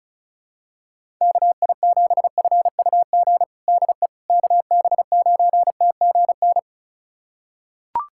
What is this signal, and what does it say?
Morse code, heard as a keyed tone, about 34 wpm, 700 Hz. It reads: KI7FUG DE KB9TGN
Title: USA Callsign DE Callsign - 6 Characters - 35wpm